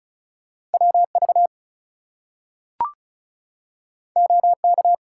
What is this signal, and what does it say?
Morse code, heard as a keyed tone, about 36 wpm, 700 Hz. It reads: WV OK
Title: US States and Canadian Provinces 35wpm